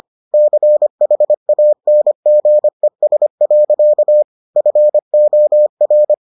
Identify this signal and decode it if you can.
Morse code, heard as a keyed tone, about 25 wpm, 600 Hz. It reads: CHANGES. FOR